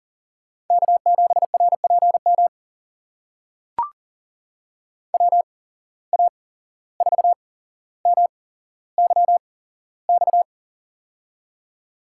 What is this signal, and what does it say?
Morse code, heard as a keyed tone, about 40 wpm, spaced as for 16 wpm, 700 Hz. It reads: K7RPM WA4MYX
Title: USA Call Signs - 8x Character Spacing - 40wpm